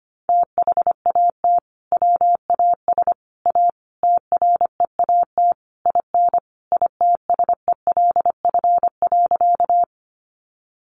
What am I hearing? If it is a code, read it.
Morse code, heard as a keyed tone, about 25 wpm, 700 Hz. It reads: THAT WAS A TREAT IN ITSELF.
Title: Sentences from Top 1300 Words 25wpm